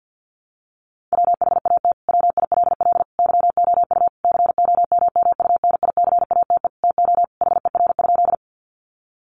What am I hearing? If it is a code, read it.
Morse code, heard as a keyed tone, about 48 wpm, 700 Hz. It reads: WHAT WILL YOU COMMUNICATE TO HER?
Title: Sentences from Top 1300 Words 50wpm